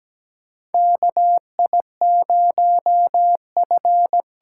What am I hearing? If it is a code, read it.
Morse code, heard as a keyed tone, about 17 wpm, 700 Hz. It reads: KI0F
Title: USA Call Signs - 8x Character Spacing - 17wpm